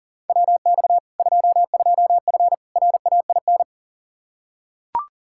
Morse code, heard as a keyed tone, about 40 wpm, 700 Hz. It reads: WX 22F RAIN
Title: Practice QSO - Exchange 2 - Info - 40wpm